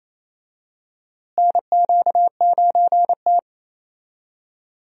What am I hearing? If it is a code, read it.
Morse code, heard as a keyed tone, about 28 wpm, 700 Hz. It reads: NQ9T
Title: USA Call Signs - 4 Characters - Repeated 2x - 28wpm